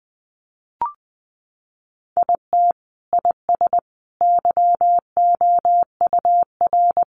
Morse code, heard as a keyed tone, about 20 wpm, 700 Hz. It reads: IT IS YOUR
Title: Sentences from Top 1100 Words 20wpm